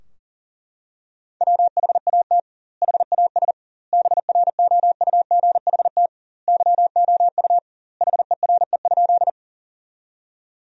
Morse code, heard as a keyed tone, about 37 wpm, 700 Hz. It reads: WHAT HAS BROUGHT YOU HERE?
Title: 2x Word-Spacing - Sentences from Top 500 Words - 40wpm